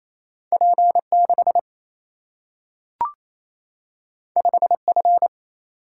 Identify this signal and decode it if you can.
Morse code, heard as a keyed tone, about 28 wpm, 700 Hz. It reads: P6 5F
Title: Single-Letter Single-Number 28wpm